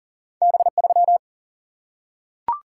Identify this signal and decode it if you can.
Morse code, heard as a keyed tone, about 39 wpm, 700 Hz. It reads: B3
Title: Single-Letter Single-Number 40wpm